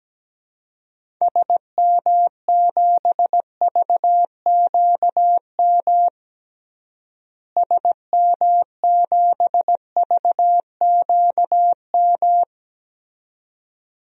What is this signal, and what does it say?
Morse code, heard as a keyed tone, about 17 wpm, 700 Hz. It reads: SM7VQM SM7VQM
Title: DXCC-100 Call Signs - 6 Characters - Repeated 2x - 17wpm